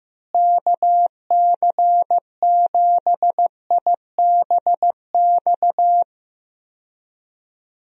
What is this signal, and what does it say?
Morse code, heard as a keyed tone, about 15 wpm, 700 Hz. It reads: KC7IBX
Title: USA Call Signs - 6 Characters - Repeated 3x - 15wpm